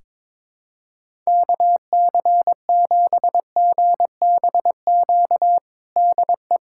Morse code, heard as a keyed tone, about 22 wpm, 700 Hz. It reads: KC7GBQ DE
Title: USA Callsign DE Callsign - 6 Characters - 22wpm